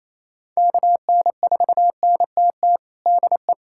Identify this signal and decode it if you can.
Morse code, heard as a keyed tone, about 28 wpm, 700 Hz. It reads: KN4NTT DE
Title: USA Callsign DE Callsign 28wpm